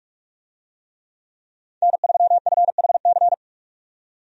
Morse code, heard as a keyed tone, about 47 wpm, 700 Hz. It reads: N3FHC
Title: USA Call Signs - 5 Characters - 45wpm